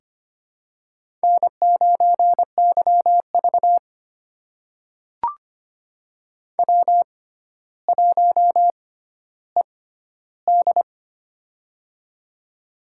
Morse code, heard as a keyed tone, about 25 wpm, spaced as for 13 wpm, 700 Hz. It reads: N9YV W1ED
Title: USA Call Signs - 4 Characters - 6x Character Spacing - 25wpm